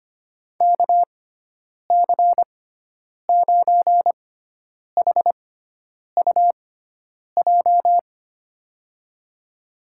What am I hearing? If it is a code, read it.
Morse code, heard as a keyed tone, about 25 wpm, spaced as for 9 wpm, 700 Hz. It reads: KC9HUJ
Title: USA Call Signs - 6 Characters - 6x Character Spacing - 25wpm